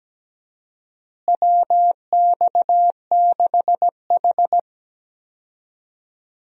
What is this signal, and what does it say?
Morse code, heard as a keyed tone, about 17 wpm, 700 Hz. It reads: WX6H
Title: USA Call Signs - 4 Characters - Repeated 2x - 17wpm